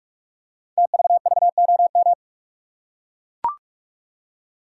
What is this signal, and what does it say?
Morse code, heard as a keyed tone, about 44 wpm, 690 Hz. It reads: TVVOM
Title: Code Groups - 5 Letters - 45wpm